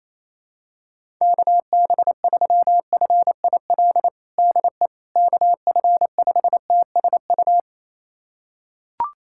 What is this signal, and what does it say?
Morse code, heard as a keyed tone, about 28 wpm, 700 Hz. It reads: KB3FIL DE KF5TSU
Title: USA Callsign DE Callsign - 6 Characters - 28wpm